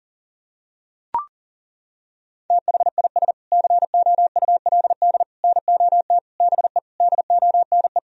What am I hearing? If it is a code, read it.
Morse code, heard as a keyed tone, about 40 wpm, 700 Hz. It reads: THIS COULD NOT BE DONE
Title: Sentences from Top 1000 Words - Review Part 3 - 40wpm